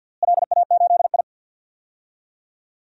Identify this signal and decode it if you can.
Morse code, heard as a keyed tone, about 45 wpm, 700 Hz. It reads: LA8I
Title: DXCC-100 Call Signs - 4 Characters - Repeated 2x - 50wpm